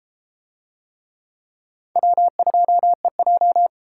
Morse code, heard as a keyed tone, about 31 wpm, 700 Hz. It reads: W2EJ